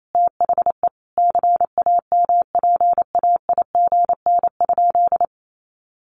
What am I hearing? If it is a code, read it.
Morse code, heard as a keyed tone, about 28 wpm, 700 Hz. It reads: THE CAMPAIGN?
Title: Sentences from Top 1200 Words 28wpm